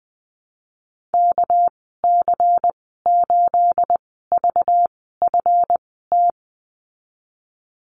Morse code, heard as a keyed tone, about 20 wpm, spaced as for 15 wpm, 700 Hz. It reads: KC8VFT
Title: 238 Lesson 38g - Letter q - Call Signs 20wpm